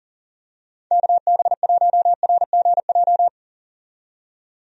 Morse code, heard as a keyed tone, about 40 wpm, 700 Hz. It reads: KB1RGJ
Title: USA Call Signs - 6 Characters - 40wpm